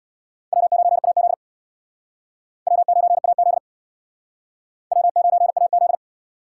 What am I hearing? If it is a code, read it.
Morse code, heard as a keyed tone, about 62 wpm, 700 Hz. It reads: W9AZ W9AZ W9AZ
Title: Speed Racing - USA 1x2 Call Signs 50wpm